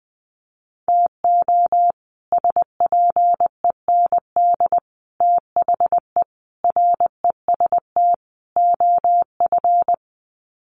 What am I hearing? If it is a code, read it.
Morse code, heard as a keyed tone, about 20 wpm, 700 Hz. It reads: TO SPEND THE REST OF
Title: Sets of 5 Words Encore2 20wpm